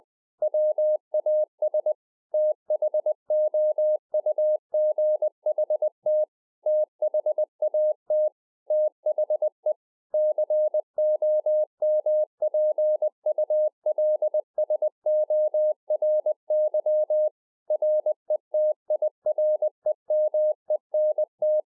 Morse code, heard as a keyed tone, about 20 wpm, 610 Hz. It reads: WAS THOUGHT THAT THE COMPULSORY RETIREMENT
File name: 20WPM